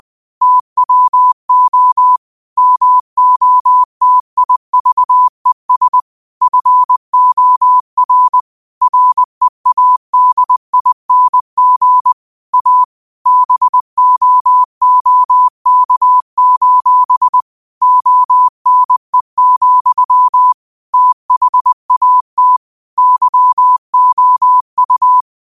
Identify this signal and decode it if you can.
Morse code, heard as a keyed tone, about 20 wpm, 1000 Hz. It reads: TWO MOTIVES FOR READING A BOOK: ONE, THAT YOU
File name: qotdfiles / 20wpm